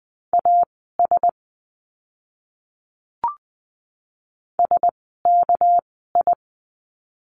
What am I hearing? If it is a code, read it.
Morse code, heard as a keyed tone, about 20 wpm, spaced as for 15 wpm, 700 Hz. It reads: AS SKI
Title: 199 Lesson 32f - Letter k - Words 20wpm